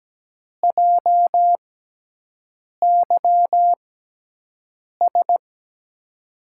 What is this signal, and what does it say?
Morse code, heard as a keyed tone, about 17 wpm, spaced as for 6 wpm, 700 Hz. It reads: JYS
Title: USA Call Signs - 6 Characters - 6x Character Spacing - 17wpm